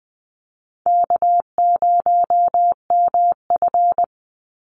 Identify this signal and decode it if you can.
Morse code, heard as a keyed tone, about 19 wpm, 700 Hz. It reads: K0MF
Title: USA Call Signs 20wpm